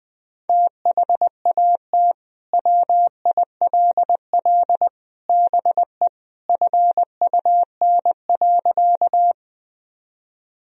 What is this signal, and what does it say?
Morse code, heard as a keyed tone, about 20 wpm, 700 Hz. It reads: THAT WILL BE FUN.
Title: Sentences from Top 1000 Words - Review Part 4 - 20wpm